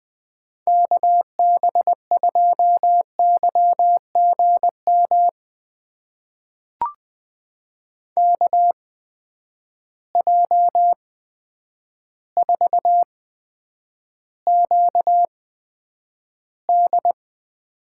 Morse code, heard as a keyed tone, about 20 wpm, spaced as for 9 wpm, 700 Hz. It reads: KB2YGM KJ4QD
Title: USA Call Signs - 6 Characters - 8x Character Spacing - 20wpm